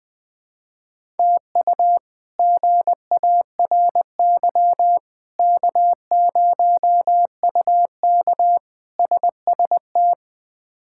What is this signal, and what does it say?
Morse code, heard as a keyed tone, about 21 wpm, 700 Hz. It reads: TU GARY K0UK SST
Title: K1USN SST QSO Practice - 20wpm